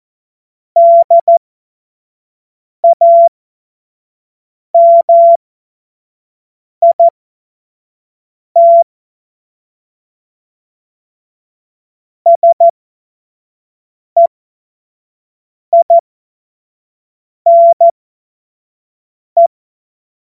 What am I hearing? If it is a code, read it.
Morse code, heard as a keyed tone, about 14 wpm, spaced as for 5 wpm, 680 Hz. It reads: DAMIT SEINE
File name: Morsekurs / Witze / Witz 14. Februar 2023 25 F70